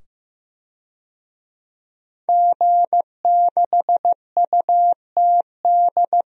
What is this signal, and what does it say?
Morse code, heard as a keyed tone, about 15 wpm, 700 Hz. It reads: G6UTD
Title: DXCC-100 Call Signs - 5 Characters - 15wpm